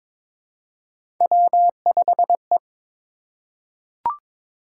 Morse code, heard as a keyed tone, about 22 wpm, 700 Hz. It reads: W5E
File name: practice / 22wpm / Three Letter-Number Combinations - 22wpm